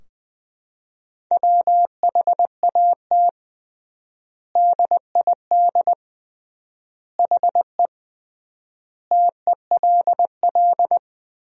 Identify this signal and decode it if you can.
Morse code, heard as a keyed tone, about 21 wpm, 700 Hz. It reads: WHAT DID HE TELL
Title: 3x Word-Spacing - Sentences from Top 200 Words - 20wpm